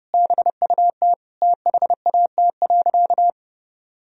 Morse code, heard as a keyed tone, about 30 wpm, 700 Hz. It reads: BUT THAT.
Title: Sentences from Top 1000 Words - Review Part 3 - 30wpm